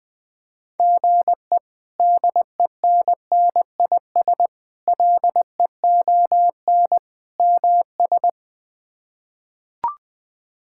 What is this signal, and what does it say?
Morse code, heard as a keyed tone, about 20 wpm, 700 Hz. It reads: GE DENNIS LEON MS
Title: K1USN SST QSO Practice - 20wpm